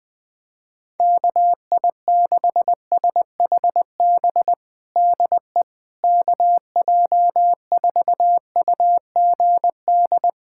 Morse code, heard as a keyed tone, about 20 wpm, 700 Hz. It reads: KI6SHB DE KJ4UGD
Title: USA Callsign DE Callsign 20wpm